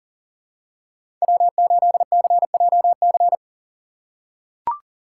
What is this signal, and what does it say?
Morse code, heard as a keyed tone, about 39 wpm, 700 Hz. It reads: W8CJC
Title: USA Call Signs - 5 Characters - 6x Character Spacing - 40wpm